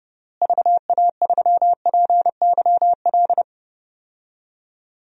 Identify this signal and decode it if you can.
Morse code, heard as a keyed tone, about 30 wpm, 700 Hz. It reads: VA3PYL
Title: DXCC-100 Call Signs - 6 Characters - Repeated 3x - 30wpm